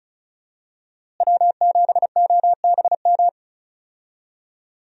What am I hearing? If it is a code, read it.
Morse code, heard as a keyed tone, about 36 wpm, 700 Hz. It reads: W7OBM
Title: USA Call Signs - 5 Characters - Repeated 2x - 35wpm